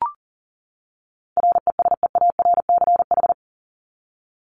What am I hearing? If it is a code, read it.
Morse code, heard as a keyed tone, about 40 wpm, 700 Hz. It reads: RESEARCH
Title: Top 501-750 Words 40wpm